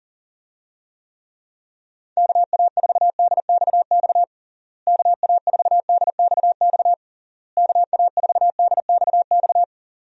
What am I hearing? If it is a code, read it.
Morse code, heard as a keyed tone, about 40 wpm, 700 Hz. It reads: KA4DXX KA4DXX KA4DXX
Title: USA Call Signs - 6 Characters - Repeated 3x - 40wpm